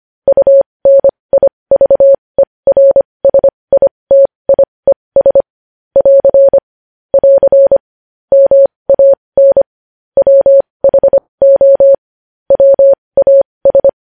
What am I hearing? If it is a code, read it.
Morse code, heard as a keyed tone, about 25 wpm, spaced as for 20 wpm, 550 Hz. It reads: UNIVERSITIES <AR> <AR> MAN WHO WAS
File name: podcast / News headlines-1x-25x20